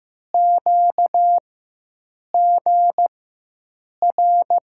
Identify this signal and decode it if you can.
Morse code, heard as a keyed tone, about 15 wpm, spaced as for 7 wpm, 700 Hz. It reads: QGR
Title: USA Call Signs - 6 Characters - 4x Character Spacing - 15wpm